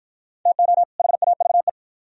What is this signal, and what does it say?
Morse code, heard as a keyed tone, about 49 wpm, 700 Hz. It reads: TO HAVE